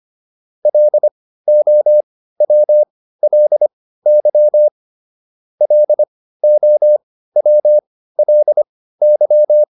Morse code, heard as a keyed tone, about 25 wpm, spaced as for 15 wpm, 600 Hz. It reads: LOWLY LOWLY